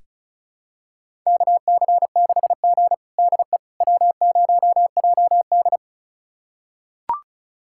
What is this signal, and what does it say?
Morse code, heard as a keyed tone, about 36 wpm, 700 Hz. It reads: KC6G DE W0JD
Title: USA Callsign DE Callsign - 4 Characters - 35wpm